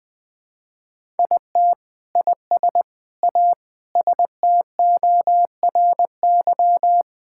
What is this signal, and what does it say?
Morse code, heard as a keyed tone, about 20 wpm, 700 Hz. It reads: IT IS A STORY
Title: Sentences from Top 1000 Words - Review Part 5 - 20wpm